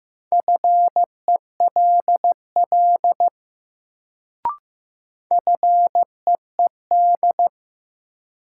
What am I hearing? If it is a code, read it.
Morse code, heard as a keyed tone, about 15 wpm, 700 Hz. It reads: FELL FEED
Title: Sentences from Top 1600 Words - New Word Drill Alt - 15wpm